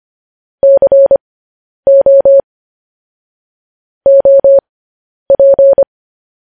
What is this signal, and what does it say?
Morse code, heard as a keyed tone, about 25 wpm, spaced as for 10 wpm, 550 Hz. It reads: CO OP
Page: news-headlines-in-morse-code-at-10-wpm